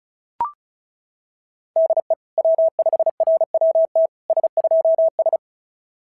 Morse code, heard as a keyed tone, about 36 wpm, 650 Hz. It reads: DE W5RWT S2S
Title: SOTA QSO Practice 35wpm